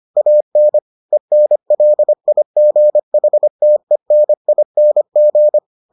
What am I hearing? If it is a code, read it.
Morse code, heard as a keyed tone, about 25 wpm, 600 Hz. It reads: AN ENLIGHTENING